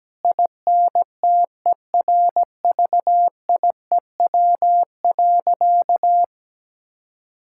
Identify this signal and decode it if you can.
Morse code, heard as a keyed tone, about 17 wpm, 700 Hz. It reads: INTERVIEW.
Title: Sentences from Top 1600 Words 17wpm